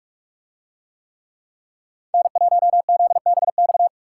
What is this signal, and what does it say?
Morse code, heard as a keyed tone, about 43 wpm, 700 Hz. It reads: N1ZBX